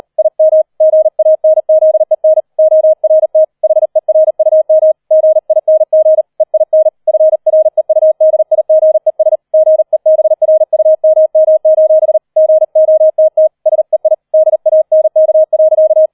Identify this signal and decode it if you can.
Morse code, heard as a keyed tone, about 38 wpm, 630 Hz. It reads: IM GANZEN ORT HERUM GING EIN FREUDIGES GEBRUMM: GOTT SEI DANK.